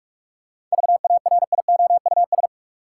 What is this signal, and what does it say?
Morse code, heard as a keyed tone, about 44 wpm, 700 Hz. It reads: VARIOUS
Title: Seven Letter Words - 45wpm